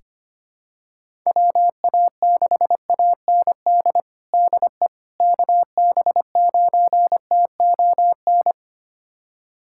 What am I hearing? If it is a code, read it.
Morse code, heard as a keyed tone, about 25 wpm, 700 Hz. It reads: WA6AND DE KB9TON
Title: USA Callsign DE Callsign 25wpm